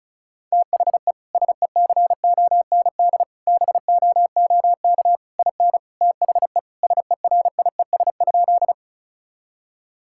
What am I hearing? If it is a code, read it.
Morse code, heard as a keyed tone, about 36 wpm, 700 Hz. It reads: THE SECOND BOOK IN THE SERIES?
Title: Sentences from Top 1000 Words - Review Part 5 - 35wpm